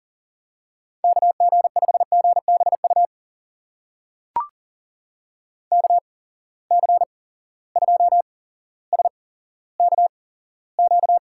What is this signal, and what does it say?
Morse code, heard as a keyed tone, about 41 wpm, spaced as for 17 wpm, 700 Hz. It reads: KG5GBU KC2SKQ
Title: USA Call Signs - 6 Characters - 8x Character Spacing - 40wpm